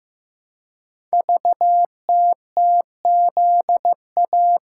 Morse code, heard as a keyed tone, about 15 wpm, 700 Hz. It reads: VTTZA
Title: Code Groups - 5 Letters - 15wpm